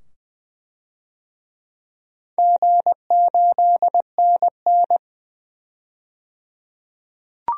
Morse code, heard as a keyed tone, about 20 wpm, 700 Hz. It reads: G8NN
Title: DXCC-100 Call Signs - 4 Characters - 20wpm